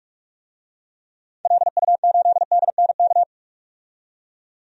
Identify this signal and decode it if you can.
Morse code, heard as a keyed tone, about 47 wpm, 690 Hz. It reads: LU8DNK
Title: DXCC-100 Call Signs - 6 Characters - 45wpm